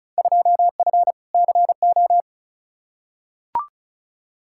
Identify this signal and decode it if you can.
Morse code, heard as a keyed tone, about 36 wpm, 700 Hz. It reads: 2F CO
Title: Field Day Exchanges - 35wpm